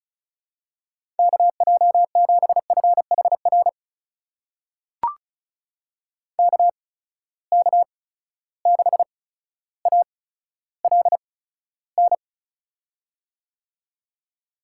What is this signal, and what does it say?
Morse code, heard as a keyed tone, about 36 wpm, spaced as for 15 wpm, 700 Hz. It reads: KJ7FHR KK6ALN
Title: USA Call Signs - 6 Characters - 8x Character Spacing - 35wpm